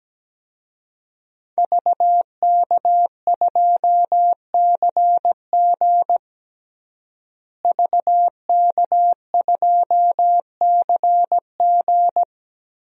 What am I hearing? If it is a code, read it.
Morse code, heard as a keyed tone, about 17 wpm, 700 Hz. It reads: VK2CG VK2CG
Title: DXCC-100 Call Signs - 5 Characters - Repeated 2x - 17wpm